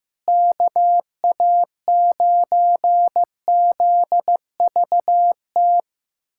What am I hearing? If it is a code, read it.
Morse code, heard as a keyed tone, about 14 wpm, 700 Hz. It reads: KA9ZVT
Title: USA Callsign DE Callsign - 6 Characters - 15wpm